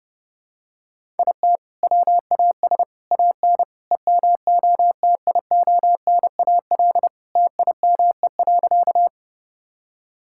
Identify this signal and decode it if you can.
Morse code, heard as a keyed tone, about 31 wpm, 700 Hz. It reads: IT WAS AN EMOTIONAL TIME.
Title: Sentences from Top 1000 Words - Review Part 5 - 30wpm